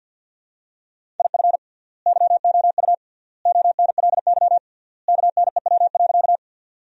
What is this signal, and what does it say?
Morse code, heard as a keyed tone, about 50 wpm, 700 Hz. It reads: IF YOU ONLY KNEW.
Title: 3x Word-Spacing - Sentences from Top 400 Words - 50wpm